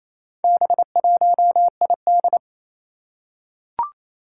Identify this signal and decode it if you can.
Morse code, heard as a keyed tone, about 28 wpm, 700 Hz. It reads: B1ID
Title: Four Challenging Letter-Number Combinations 28wpm